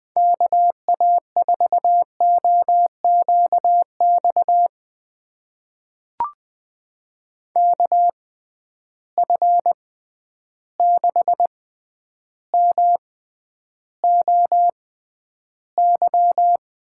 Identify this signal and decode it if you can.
Morse code, heard as a keyed tone, about 20 wpm, spaced as for 10 wpm, 700 Hz. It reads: KA4OQX KF6MOY
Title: USA Call Signs - 6 Characters - 6x Character Spacing - 20wpm